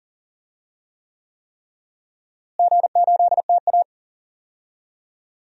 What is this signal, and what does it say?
Morse code, heard as a keyed tone, about 40 wpm, 700 Hz. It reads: G8TA
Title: DXCC-100 Call Signs - 4 Characters - Repeated 3x - 40wpm